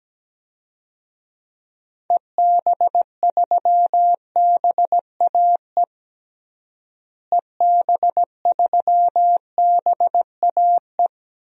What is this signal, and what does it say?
Morse code, heard as a keyed tone, about 17 wpm, 700 Hz. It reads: EB3BAE EB3BAE
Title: DXCC-100 Call Signs - 6 Characters - Repeated 3x - 17wpm